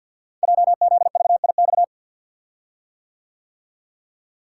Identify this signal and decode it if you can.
Morse code, heard as a keyed tone, about 48 wpm, 700 Hz. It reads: JZVIX
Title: Code Groups - 5 Letters - 50wpm